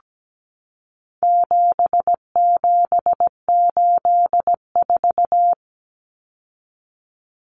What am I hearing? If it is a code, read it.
Morse code, heard as a keyed tone, about 17 wpm, 700 Hz. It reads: 7784
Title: Common Cut Numbers - 4-Digit 17wpm